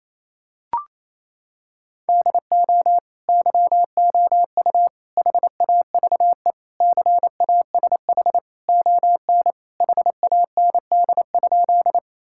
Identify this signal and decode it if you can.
Morse code, heard as a keyed tone, about 28 wpm, 700 Hz. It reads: DO YOU HAVE CASH ON HAND?
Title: Sentences from Top 1000 Words - Review Part 5 - 28wpm